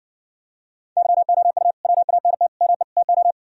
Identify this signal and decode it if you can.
Morse code, heard as a keyed tone, about 59 wpm, 700 Hz. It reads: YOU WANT ME TO